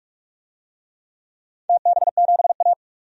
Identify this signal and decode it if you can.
Morse code, heard as a keyed tone, about 44 wpm, 700 Hz. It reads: TB7A
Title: DXCC-100 Call Signs - 4 Characters - 45wpm